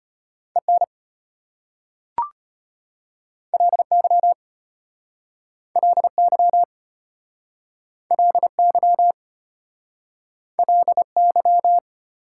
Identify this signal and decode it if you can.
Morse code, heard as a keyed tone, about 33 wpm, 700 Hz. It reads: EN LY LY LY LY